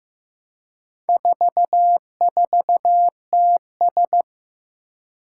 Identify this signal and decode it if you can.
Morse code, heard as a keyed tone, about 15 wpm, 700 Hz. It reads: 44TS
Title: Four Challenging Letter-Number Combinations 15wpm